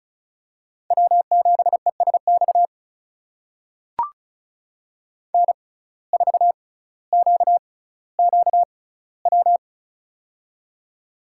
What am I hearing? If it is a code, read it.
Morse code, heard as a keyed tone, about 36 wpm, spaced as for 18 wpm, 700 Hz. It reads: W7ESX N4QQW
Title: USA Call Signs - 5 Characters - 6x Character Spacing - 35wpm